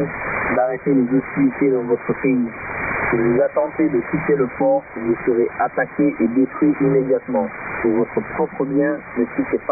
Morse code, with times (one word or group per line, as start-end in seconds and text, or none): none